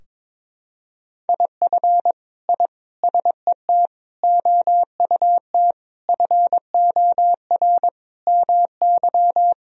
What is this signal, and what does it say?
Morse code, heard as a keyed tone, about 22 wpm, 700 Hz. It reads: IF I SET OUT FOR MY
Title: Sentences from Top 1000 Words - Review Part 2 - 22wpm